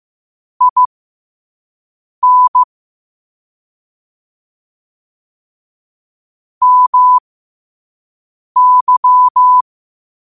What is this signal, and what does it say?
Morse code, heard as a keyed tone, about 15 wpm, spaced as for 5 wpm, 1000 Hz. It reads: IN MY
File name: qotdfiles / 5wpm